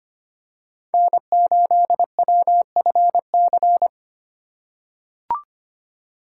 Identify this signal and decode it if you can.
Morse code, heard as a keyed tone, about 25 wpm, 700 Hz. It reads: N8WFC
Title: USA Call Signs - 5 Characters - 3x Character Spacing - 25wpm